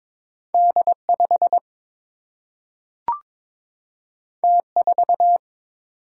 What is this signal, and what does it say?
Morse code, heard as a keyed tone, about 22 wpm, 700 Hz. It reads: D 5 T 4
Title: Single-Letter Single-Number 22wpm